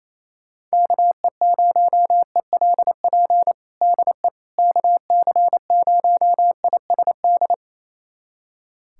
Morse code, heard as a keyed tone, about 28 wpm, 700 Hz. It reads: KE0ELP DE KC0ISD
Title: USA Callsign DE Callsign - 6 Characters - 28wpm